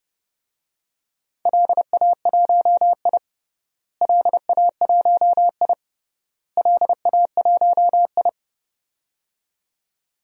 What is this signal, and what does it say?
Morse code, heard as a keyed tone, about 30 wpm, 700 Hz. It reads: LA1I LA1I LA1I
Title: DXCC-100 Call Signs - 4 Characters - Repeated 3x - 30wpm